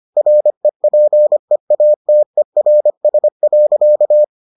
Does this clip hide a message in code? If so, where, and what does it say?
Yes: Morse code from 0.2 to 4.2 s, heard as a keyed tone, about 25 wpm, 600 Hz. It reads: REPEATERS.